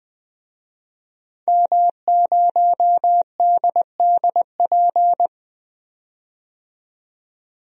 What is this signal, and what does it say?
Morse code, heard as a keyed tone, about 20 wpm, 700 Hz. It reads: M0DDP
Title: DXCC-100 Call Signs - 5 Characters - 20wpm